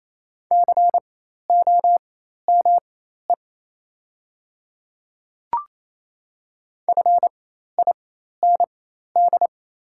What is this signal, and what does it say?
Morse code, heard as a keyed tone, about 28 wpm, spaced as for 13 wpm, 700 Hz. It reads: COME FIND
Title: 4x Character Spacing - Top 200 Words - 28wpm